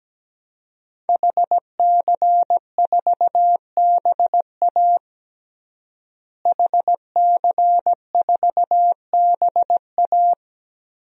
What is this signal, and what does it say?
Morse code, heard as a keyed tone, about 17 wpm, 700 Hz. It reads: HC4BA HC4BA
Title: DXCC-100 Call Signs - 5 Characters - Repeated 3x - 17wpm